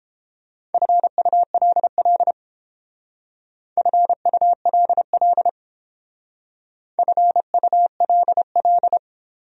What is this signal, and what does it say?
Morse code, heard as a keyed tone, about 31 wpm, 700 Hz. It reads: FULL FULL FULL